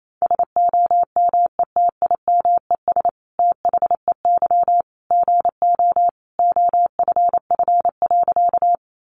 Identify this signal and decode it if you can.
Morse code, heard as a keyed tone, about 28 wpm, 700 Hz. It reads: SOMETIMES THEY GO OFF.
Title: Sentences from Top 600 Words 28wpm